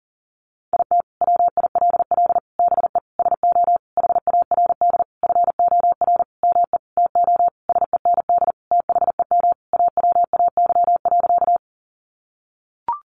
Morse code, heard as a keyed tone, about 37 wpm, 700 Hz. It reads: IT WILL BE SO HARD FOR ME TO SEND THEM AWAY.
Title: Sentences from Top 1500 Words 40wpm